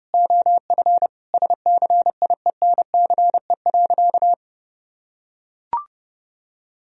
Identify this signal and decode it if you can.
Morse code, heard as a keyed tone, about 32 wpm, 700 Hz. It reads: OF SCIENCE.
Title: Sentences from Top 1000 Words - Review Part 4 - 30wpm